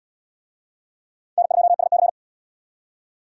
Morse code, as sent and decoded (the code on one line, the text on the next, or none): -. ..--- ... -.-.
N2SC